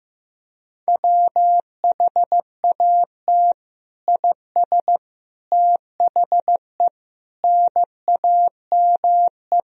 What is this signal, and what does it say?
Morse code, heard as a keyed tone, about 15 wpm, 700 Hz. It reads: WHAT IS THE NAME
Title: Sentences from Top 1000 Words - Review Part 3 - 15wpm